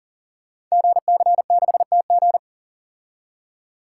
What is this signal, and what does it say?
Morse code, heard as a keyed tone, about 39 wpm, 690 Hz. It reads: GC6TG
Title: Code Groups - 5 Letter:Numbers - 40wpm